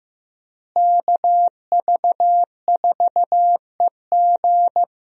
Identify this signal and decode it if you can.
Morse code, heard as a keyed tone, about 15 wpm, 700 Hz. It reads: KV4EG
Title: USA Call Signs - 5 Characters - Repeated 3x - 15wpm